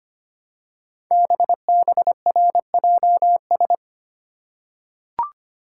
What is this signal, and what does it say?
Morse code, heard as a keyed tone, about 25 wpm, 700 Hz. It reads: BBRJS
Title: Code Groups - 5 Letters - 25wpm